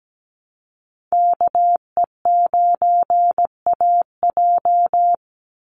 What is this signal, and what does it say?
Morse code, heard as a keyed tone, about 17 wpm, 700 Hz. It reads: KE9AJ
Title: NA SOTA Chaser Callsigns 17wpm